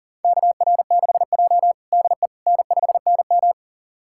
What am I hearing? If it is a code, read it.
Morse code, heard as a keyed tone, about 37 wpm, 700 Hz. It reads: KR6J DE N5NM
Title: USA Callsign DE Callsign - 4 Characters - 40wpm